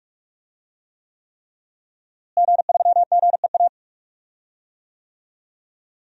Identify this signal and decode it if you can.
Morse code, heard as a keyed tone, about 47 wpm, 700 Hz. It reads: G3GEA
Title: DXCC-100 Call Signs - 5 Characters - Repeated 3x - 45wpm